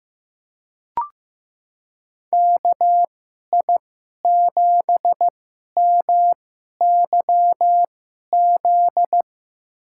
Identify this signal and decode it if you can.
Morse code, heard as a keyed tone, about 15 wpm, spaced as for 11 wpm, 700 Hz. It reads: KI7MYZ
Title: USA Call Signs - 6 Characters - 2x Character Spacing - 15wpm